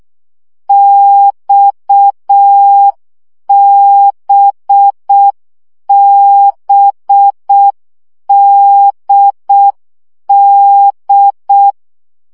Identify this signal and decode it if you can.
Morse code, heard as a keyed tone, about 6 wpm, 800 Hz. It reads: XBBDD
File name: CW gyakorlas d b x 5 baud